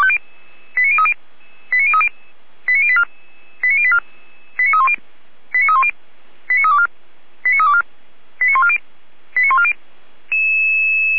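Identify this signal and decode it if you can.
Five-tone selective call (ZVEI), heard as quick runs of five tones, tones of about 70 ms. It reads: 89030 89030 89084 89084 89310 89310 89324 89324 89140 89140
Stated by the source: ZVEI five-tone selective call received with an SDR.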